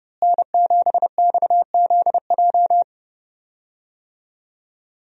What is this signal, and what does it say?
Morse code, heard as a keyed tone, about 32 wpm, 700 Hz. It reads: N7XZJ
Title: USA Call Signs - 5 Characters - Repeated 3x - 30wpm